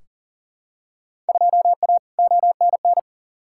Morse code, heard as a keyed tone, about 40 wpm, 700 Hz. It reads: 2A ONN
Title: Field Day Exchanges Encore - 40wpm